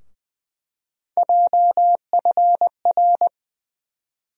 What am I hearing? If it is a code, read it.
Morse code, heard as a keyed tone, about 21 wpm, 700 Hz. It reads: JFR
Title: USA Call Sign Postfixes - 3 Letters 20wpm